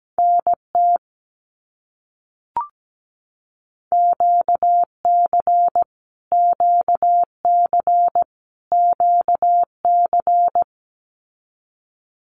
Morse code, heard as a keyed tone, about 17 wpm, 700 Hz. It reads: NT QC QC QC
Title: Canadian Provinces and Territories - Repeated 3x - 17wpm